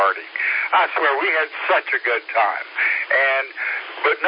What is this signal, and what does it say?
Emergency Alert System (SAME) tones, heard as harsh buzzing bursts, with speech. It is heard from 0.2 to 4.0 s.